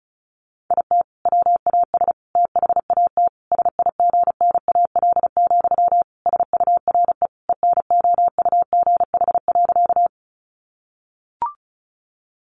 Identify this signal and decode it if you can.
Morse code, heard as a keyed tone, about 34 wpm, 700 Hz. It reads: IT WAS THAT SIGNAL, SURE ENOUGH.
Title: Sentences from Top 1400 Words 35wpm